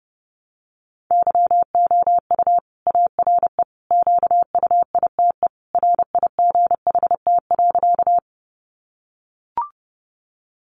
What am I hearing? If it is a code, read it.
Morse code, heard as a keyed tone, about 30 wpm, 700 Hz. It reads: YOU ARE QUITE RIGHT.
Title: Sentences from Top 700 Words 30wpm